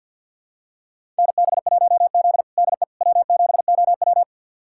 Morse code, heard as a keyed tone, about 48 wpm, 700 Hz. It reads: NB1Z DE W7OW
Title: USA Callsign DE Callsign - 4 Characters - 50wpm